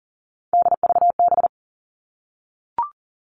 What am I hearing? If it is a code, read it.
Morse code, heard as a keyed tone, about 40 wpm, 700 Hz. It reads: DVB
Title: All Cut Numbers - 3-Digit 40wpm